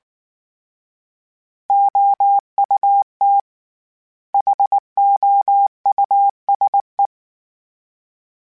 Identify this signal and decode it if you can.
Morse code, heard as a keyed tone, about 19 wpm, 800 Hz. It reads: OUT HOUSE
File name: General / General-mp3s / Houses @18